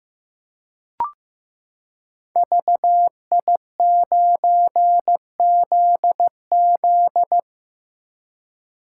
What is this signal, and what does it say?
Morse code, heard as a keyed tone, about 15 wpm, 700 Hz. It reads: VI9ZZ
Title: Code Groups - 5 Letter:Numbers - 15wpm